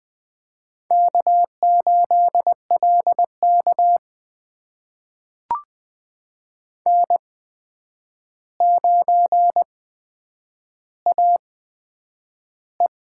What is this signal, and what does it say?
Morse code, heard as a keyed tone, about 20 wpm, spaced as for 9 wpm, 700 Hz. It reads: K8LK N9AE
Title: USA Call Signs - 4 Characters - 8x Character Spacing - 20wpm